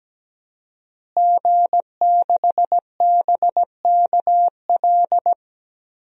Morse code, heard as a keyed tone, about 17 wpm, 700 Hz. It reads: G6BKL
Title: DXCC-100 Call Signs - 5 Characters - Repeated 3x - 17wpm